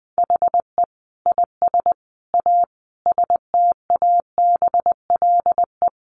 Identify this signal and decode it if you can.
Morse code, heard as a keyed tone, about 19 wpm, 700 Hz. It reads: HE IS A STABLE
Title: Sentences from Top 1500 Words 20wpm